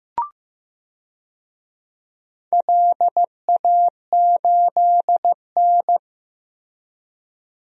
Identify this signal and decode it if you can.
Morse code, heard as a keyed tone, about 15 wpm, 700 Hz. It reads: LA8N
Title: DXCC-100 Call Signs - 4 Characters - Repeated 3x - 15wpm